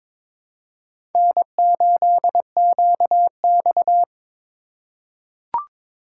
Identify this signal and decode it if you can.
Morse code, heard as a keyed tone, about 22 wpm, 700 Hz. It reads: N8QX
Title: USA Call Signs - 4 Characters - 3x Character Spacing - 22wpm